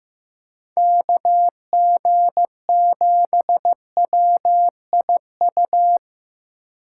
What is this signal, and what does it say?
Morse code, heard as a keyed tone, about 15 wpm, 700 Hz. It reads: KG7WIU
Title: USA Call Signs - 6 Characters - 8x Character Spacing - 15wpm